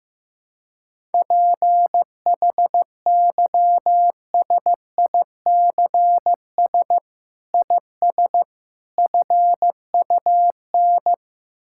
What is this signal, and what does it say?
Morse code, heard as a keyed tone, about 15 wpm, 700 Hz. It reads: PHYSICS IS FUN
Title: Sentences from Top 1000 Words - Review Part 5 - 15wpm